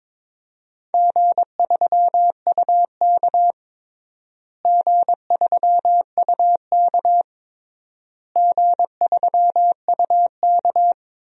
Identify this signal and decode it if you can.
Morse code, heard as a keyed tone, about 22 wpm, 700 Hz. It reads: G3UK G3UK G3UK
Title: DXCC-100 Call Signs - 4 Characters - Repeated 3x - 22wpm